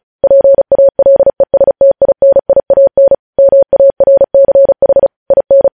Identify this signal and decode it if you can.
Morse code, heard as a keyed tone, about 36 wpm, 550 Hz. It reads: PALESTINIAN MARCH IN